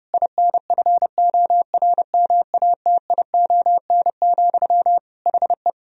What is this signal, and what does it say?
Morse code, heard as a keyed tone, about 30 wpm, 700 Hz. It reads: INFORMATION, HE
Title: Sentences from Top 1000 Words - Review Part 3 - 30wpm